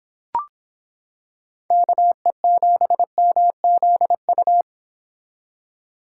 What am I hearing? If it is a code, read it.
Morse code, heard as a keyed tone, about 26 wpm, 700 Hz. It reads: KE7MZU